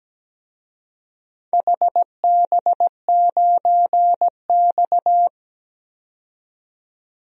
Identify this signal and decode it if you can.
Morse code, heard as a keyed tone, about 17 wpm, 700 Hz. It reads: HB9X
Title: DXCC-100 Call Signs - 4 Characters - 17wpm